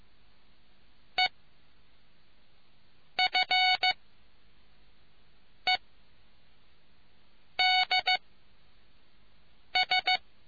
Morse code, heard as a keyed tone, about 15 wpm, 740 Hz. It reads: E F E D S